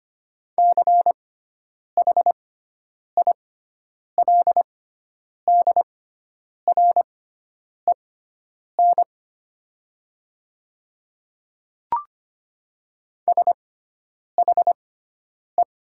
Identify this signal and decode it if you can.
Morse code, heard as a keyed tone, about 25 wpm, spaced as for 9 wpm, 700 Hz. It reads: CHILDREN SHE
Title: 6x Character Spacing - Top 300 Words - 25wpm